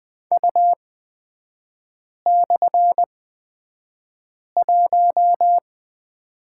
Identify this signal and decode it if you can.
Morse code, heard as a keyed tone, about 20 wpm, spaced as for 5 wpm, 700 Hz. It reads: U/1
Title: 243 Lesson 39e - Letter z - Three Characters 20wpm